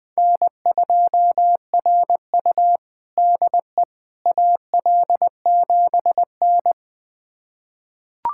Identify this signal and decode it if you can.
Morse code, heard as a keyed tone, about 21 wpm, 700 Hz. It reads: N2RU DE AL7N
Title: USA Callsign DE Callsign - 4 Characters - 20wpm